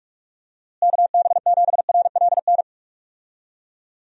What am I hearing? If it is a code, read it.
Morse code, heard as a keyed tone, about 42 wpm, 700 Hz. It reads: KB7RLN
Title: USA Call Signs - 6 Characters - Repeated 2x - 45wpm